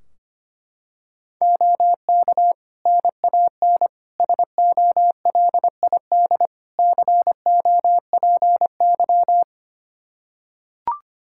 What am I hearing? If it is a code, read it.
Morse code, heard as a keyed tone, about 25 wpm, 700 Hz. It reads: OK NAN SOLID COPY
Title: Practice QSO - Exchange 3 - More Info - 25wpm